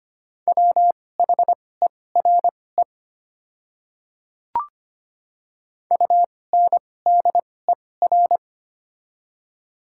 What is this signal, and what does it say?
Morse code, heard as a keyed tone, about 25 wpm, spaced as for 18 wpm, 700 Hz. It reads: WHERE UNDER
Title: 2x Character Spacing - Top 200 Words - 25wpm